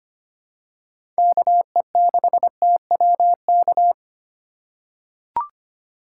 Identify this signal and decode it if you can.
Morse code, heard as a keyed tone, about 25 wpm, 700 Hz. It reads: KE6TWK
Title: USA Call Signs - 6 Characters - 4x Character Spacing - 25wpm